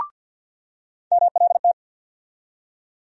Morse code, heard as a keyed tone, about 50 wpm, 700 Hz. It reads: MLT
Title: USA Call Sign Postfixes - 3 Letters 50wpm